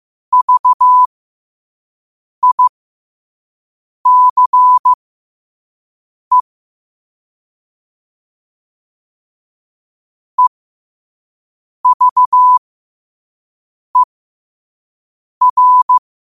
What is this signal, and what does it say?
Morse code, heard as a keyed tone, about 15 wpm, spaced as for 5 wpm, 1000 Hz. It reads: VICE EVER
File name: qotdfiles / 5wpm